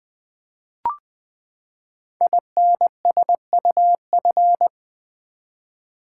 Morse code, heard as a keyed tone, about 20 wpm, 700 Hz. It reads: INSUF
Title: Build A Word - 5 Letters At A Time - 12-Letter Words - 20wpm